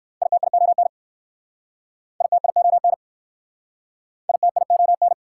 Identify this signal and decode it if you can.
Morse code, heard as a keyed tone, about 59 wpm, 690 Hz. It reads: ITION ITION ITION